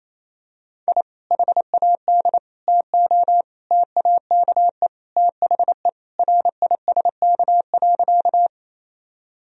Morse code, heard as a keyed tone, about 28 wpm, 700 Hz. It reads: I HAD TO TAKE THE RISK.
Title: Sentences from Top 1000 Words - Review Part 4 - 28wpm